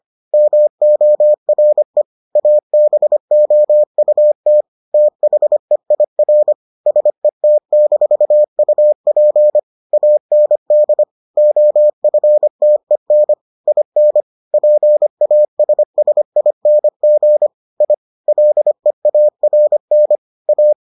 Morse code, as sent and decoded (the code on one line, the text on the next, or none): -- --- .-. / .- -... --- ..- - / - .... .. .-. / ... - -....- ..- .--. / .- -. -.. / --- ..-. - . -. / .. -. / .--. .- ... ... .. -. --. / .. / .-.. .- .-. -. / .-
MORE ABOUT THEIR SET-UP AND OFTEN IN PASSING I LEARN A